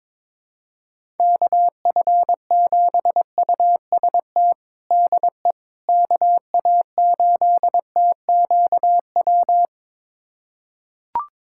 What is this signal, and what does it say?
Morse code, heard as a keyed tone, about 22 wpm, 700 Hz. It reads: KF7UST DE KA8TQW
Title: USA Callsign DE Callsign 22wpm